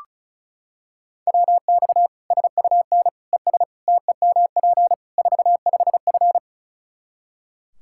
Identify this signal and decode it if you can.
Morse code, heard as a keyed tone, about 34 wpm, 700 Hz. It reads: WX SUN ES TEMP 45F
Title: Practice QSO - Exchange 2 - Info - 35wpm